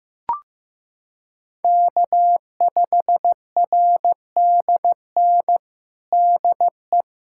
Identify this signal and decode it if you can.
Morse code, heard as a keyed tone, about 15 wpm, 700 Hz. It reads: K5RDN DE
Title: USA Callsign DE Callsign 15wpm